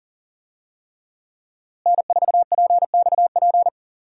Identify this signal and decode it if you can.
Morse code, heard as a keyed tone, about 39 wpm, 700 Hz. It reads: N4PXP
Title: USA Call Signs - 5 Characters - Repeated 3x - 40wpm